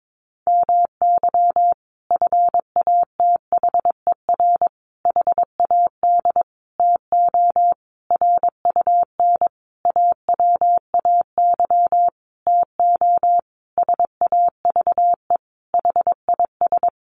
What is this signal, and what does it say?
Morse code, heard as a keyed tone, about 22 wpm, 700 Hz. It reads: MY FATHER HAD TO RUN AWAY TO SAVE HIS
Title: Sentences from Top 1100 Words 22wpm